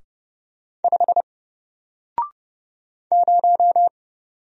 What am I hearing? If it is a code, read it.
Morse code, heard as a keyed tone, about 30 wpm, 700 Hz. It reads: 5 0
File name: Single-Letter-Number-30wpm